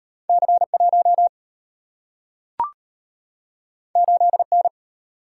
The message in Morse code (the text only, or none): C1 8N